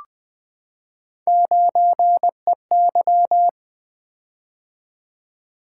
Morse code, heard as a keyed tone, about 20 wpm, 700 Hz. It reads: 9EY
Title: Three Letter-Number Combinations - 20wpm